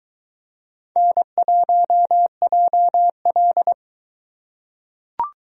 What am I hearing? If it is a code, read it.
Morse code, heard as a keyed tone, about 20 wpm, 700 Hz. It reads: N1JL